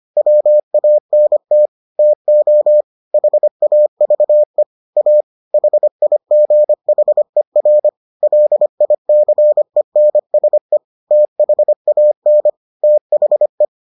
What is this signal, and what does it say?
Morse code, heard as a keyed tone, about 25 wpm, 600 Hz. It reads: WANT TO HAVE A HIGHER LICENSE THAN THE